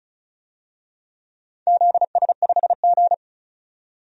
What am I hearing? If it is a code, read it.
Morse code, heard as a keyed tone, about 36 wpm, 700 Hz. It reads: ZS5G
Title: DXCC-100 Call Signs - 4 Characters - 35wpm